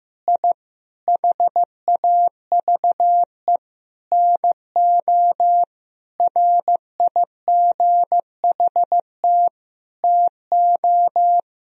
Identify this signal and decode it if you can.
Morse code, heard as a keyed tone, about 15 wpm, 700 Hz. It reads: I HAVE NO RIGHT TO
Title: Sentences from Top 1000 Words - Review Part 3 - 15wpm